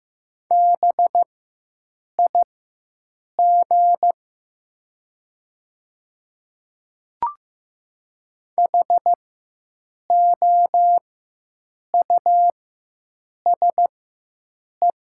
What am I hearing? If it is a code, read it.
Morse code, heard as a keyed tone, about 15 wpm, spaced as for 7 wpm, 700 Hz. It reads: BIG HOUSE
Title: 4x Character Spacing - Top 300 Words - 15wpm